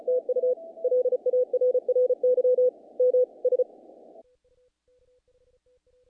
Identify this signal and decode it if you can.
Morse code, heard as a keyed tone, about 34 wpm, 500 Hz. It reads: TU LARRY MS
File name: contests / naqp-cw-k5tr-op-k5pi- -1800